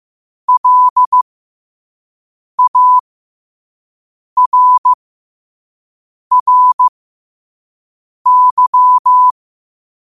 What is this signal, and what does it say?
Morse code, heard as a keyed tone, about 15 wpm, spaced as for 5 wpm, 1000 Hz. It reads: LARRY
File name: qotdfiles / 5wpm